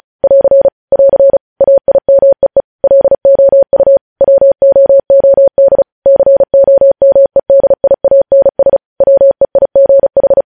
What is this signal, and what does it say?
Morse code, heard as a keyed tone, about 36 wpm, 550 Hz. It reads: <AR> <AR> AIMEE LOU WOOD COMEDIANS WEIGH
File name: podcast / News headlines-1x-35x35